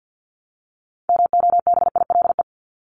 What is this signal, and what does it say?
Morse code, heard as a keyed tone, about 50 wpm, 700 Hz. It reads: MOBILE
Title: Top 1751-2000 Words 50wpm